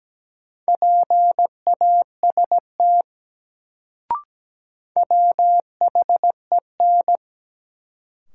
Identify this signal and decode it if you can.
Morse code, heard as a keyed tone, about 17 wpm, 700 Hz. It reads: PAST WHEN
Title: Sets of 2 Words - New Word Drill - 17wpm